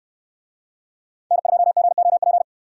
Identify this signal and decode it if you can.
Morse code, heard as a keyed tone, about 69 wpm, 700 Hz. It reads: N2GOP